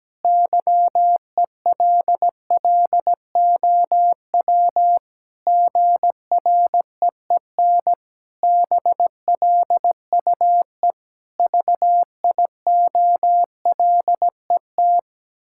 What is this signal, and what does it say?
Morse code, heard as a keyed tone, about 17 wpm, 700 Hz. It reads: YELLOW GREEN BLUE VIOLET
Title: Pangram Challenge v2 - 17wpm